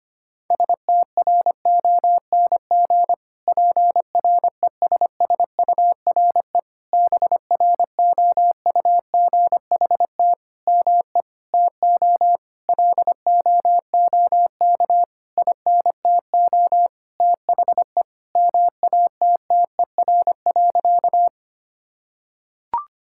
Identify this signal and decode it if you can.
Morse code, heard as a keyed tone, about 25 wpm, 700 Hz. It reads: STRONG PRESSURE BROUGHT ME TO LOOK INTO THE MATTER.
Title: Sentences from Top 1000 Words - Review Part 6 - 25wpm